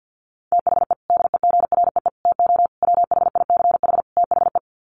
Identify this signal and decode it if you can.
Morse code, heard as a keyed tone, about 48 wpm, 700 Hz. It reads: THE DEGREE TO WHICH THE
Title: Sets of 5 Words Encore2 50wpm